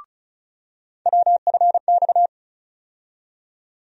Morse code, heard as a keyed tone, about 36 wpm, 700 Hz. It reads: WFX
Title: USA Call Sign Postfixes - 3 Letters 35wpm